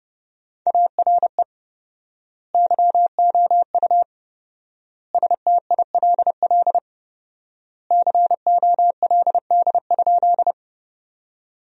Morse code, heard as a keyed tone, about 30 wpm, 700 Hz. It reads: ARE YOU STILL COLD?
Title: 4x Word-Spacing - Sentences from Top 400 Words - 30wpm